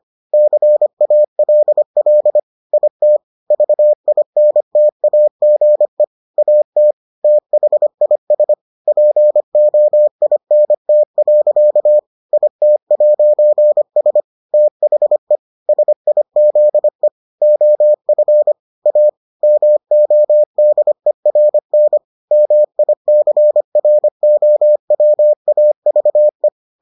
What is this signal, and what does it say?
Morse code, heard as a keyed tone, about 25 wpm, 600 Hz. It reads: CALL IT VINTAGE AT THIS POINT. IT'S THE SIZE OF A MODERN MICROWAVE